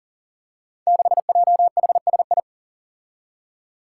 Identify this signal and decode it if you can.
Morse code, heard as a keyed tone, about 39 wpm, 700 Hz. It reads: 6JHSI
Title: Five Challenging Letter-Number Combinations 40wpm